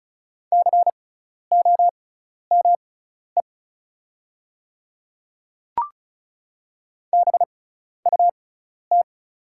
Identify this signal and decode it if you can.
Morse code, heard as a keyed tone, about 36 wpm, spaced as for 12 wpm, 700 Hz. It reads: COME BUT
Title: 6x Character Spacing - Top 200 Words - 35wpm